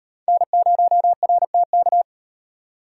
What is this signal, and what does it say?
Morse code, heard as a keyed tone, about 33 wpm, 700 Hz. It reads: N0RTK